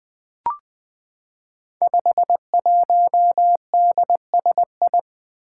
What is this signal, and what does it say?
Morse code, heard as a keyed tone, about 20 wpm, 700 Hz. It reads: 51DSI
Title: Five Challenging Letter-Number Combinations 20wpm